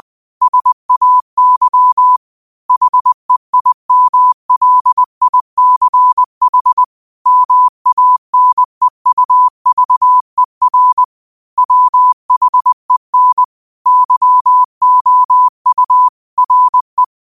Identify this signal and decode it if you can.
Morse code, heard as a keyed tone, about 20 wpm, 1000 Hz. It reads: SAY HEIMLICH MANEUVER WHEN YOU RE
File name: 20wpm